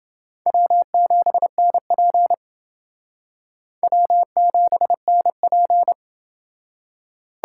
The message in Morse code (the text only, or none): W7NP W7NP